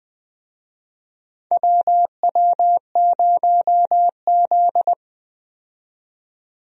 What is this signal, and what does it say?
Morse code, heard as a keyed tone, about 20 wpm, 700 Hz. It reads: WW0Z
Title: USA Call Signs - 4 Characters - 20wpm